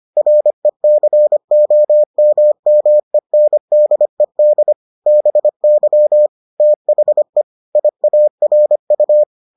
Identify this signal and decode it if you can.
Morse code, heard as a keyed tone, about 25 wpm, 600 Hz. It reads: RECOMMENDED BY THE IARU